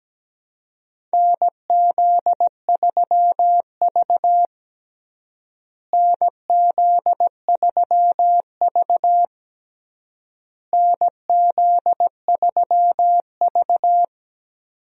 Morse code, heard as a keyed tone, about 17 wpm, 700 Hz. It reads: NZ3V NZ3V NZ3V
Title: USA Call Signs - 4 Characters - Repeated 3x - 17wpm